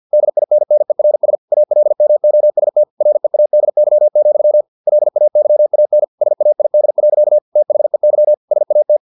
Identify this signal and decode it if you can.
Morse code, heard as a keyed tone, about 50 wpm, 600 Hz. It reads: DINNERS ALMOST READY, LAYAN SAID. THEY SAT
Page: Never 49wpm